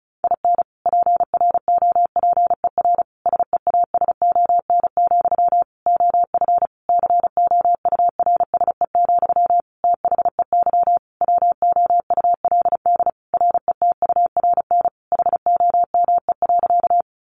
Morse code, heard as a keyed tone, about 34 wpm, 700 Hz. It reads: IN PROPER SEASON, OF COURSE, THEY WOULD RETURN HOME.
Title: Sentences from Top 1400 Words 35wpm